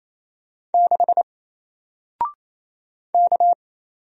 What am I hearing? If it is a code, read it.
Morse code, heard as a keyed tone, about 28 wpm, 700 Hz. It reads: 6 K
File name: Single-Letter-Number-28wpm